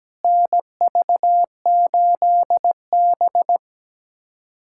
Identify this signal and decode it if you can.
Morse code, heard as a keyed tone, about 17 wpm, 700 Hz. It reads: NV8B
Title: USA Callsign DE Callsign - 4 Characters - 17wpm